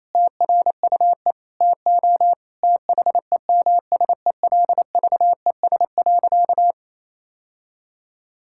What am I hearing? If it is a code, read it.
Morse code, heard as a keyed tone, about 28 wpm, 700 Hz. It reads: TRUE TO THEMSELVES.
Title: Sentences from Top 1000 Words - Review Part 6 - 28wpm